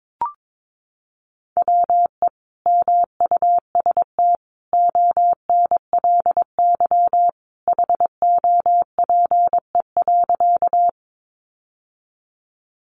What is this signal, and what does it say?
Morse code, heard as a keyed tone, about 22 wpm, 700 Hz. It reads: WE MUST ONLY HOPE.
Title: Sentences from Top 1000 Words 22wpm